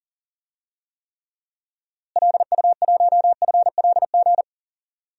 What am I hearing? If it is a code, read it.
Morse code, heard as a keyed tone, about 42 wpm, 700 Hz. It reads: LU1FLG
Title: DXCC-100 Call Signs - 6 Characters - Repeated 3x - 40wpm